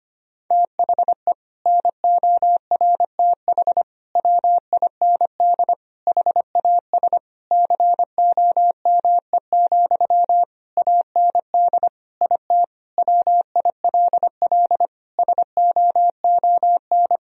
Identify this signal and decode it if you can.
Morse code, heard as a keyed tone, about 25 wpm, 700 Hz. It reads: THE NORTH WIND HAS COME, AND IT WILL SOON
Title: Sentences from Top 1000 Words - Review Part 3 - 25wpm